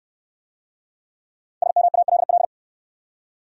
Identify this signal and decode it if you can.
Morse code, heard as a keyed tone, about 69 wpm, 700 Hz. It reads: SMALL